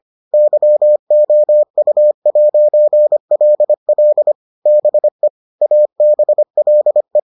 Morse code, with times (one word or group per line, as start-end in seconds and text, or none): 0.3-4.3 YOU'LL
4.7-5.3 BE
5.6-7.2 ABLE